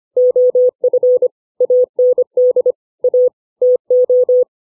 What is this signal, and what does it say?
Morse code, heard as a keyed tone, about 25 wpm, 500 Hz. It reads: OF AND A TO
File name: CW / five common words 5